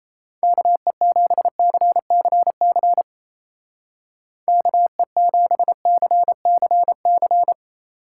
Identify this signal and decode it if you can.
Morse code, heard as a keyed tone, about 33 wpm, 700 Hz. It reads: KE7CCC KE7CCC
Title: Speed Racing - USA 2x3 Call Signs 28wpm